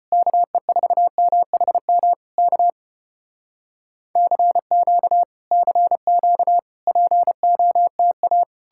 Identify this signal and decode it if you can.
Morse code, heard as a keyed tone, about 34 wpm, 700 Hz. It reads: KE4MHM K CQ CQ POTA